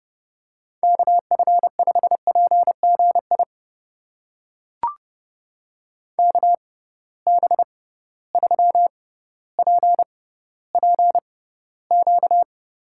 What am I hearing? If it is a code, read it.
Morse code, heard as a keyed tone, about 30 wpm, spaced as for 15 wpm, 700 Hz. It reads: KF5PGI KB3PPQ
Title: USA Call Signs - 6 Characters - 6x Character Spacing - 30wpm